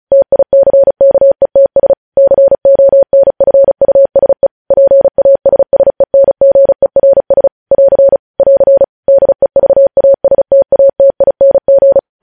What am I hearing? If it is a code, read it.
Morse code, heard as a keyed tone, about 36 wpm, 550 Hz. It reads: TICKETS CONFUSE PASSENGERS <AR> <AR> DEVASTATING